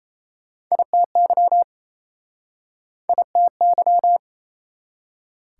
Morse code, heard as a keyed tone, about 39 wpm, 700 Hz. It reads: ITY ITY